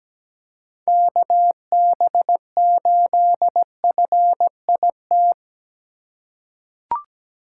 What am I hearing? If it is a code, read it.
Morse code, heard as a keyed tone, about 17 wpm, 700 Hz. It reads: KB8FIT
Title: USA Call Signs - 6 Characters - 6x Character Spacing - 17wpm